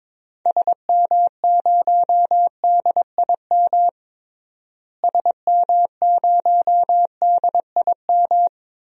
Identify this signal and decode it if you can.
Morse code, heard as a keyed tone, about 22 wpm, 700 Hz. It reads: SM0DIM SM0DIM
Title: DXCC-100 Call Signs - 6 Characters - Repeated 2x - 22wpm